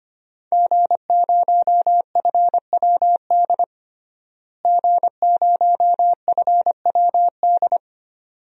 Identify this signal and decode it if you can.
Morse code, heard as a keyed tone, about 25 wpm, 700 Hz. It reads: G0FWD G0FWD
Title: DXCC-100 Call Signs - 5 Characters - Repeated 2x - 25wpm